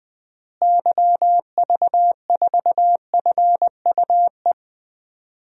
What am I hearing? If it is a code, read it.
Morse code, heard as a keyed tone, about 20 wpm, 700 Hz. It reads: YV4FUE
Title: DXCC-100 Call Signs - 6 Characters - Repeated 2x - 20wpm